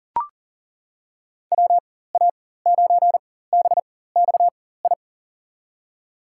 Morse code, heard as a keyed tone, about 39 wpm, spaced as for 19 wpm, 700 Hz. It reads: WA9BXI
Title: USA Call Signs - 6 Characters - 4x Character Spacing - 40wpm